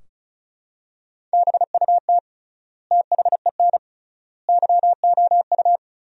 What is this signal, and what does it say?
Morse code, heard as a keyed tone, about 34 wpm, 700 Hz. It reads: BUT THEN YOU
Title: 3x Word-Spacing - Sentences from Top 300 Words - 35wpm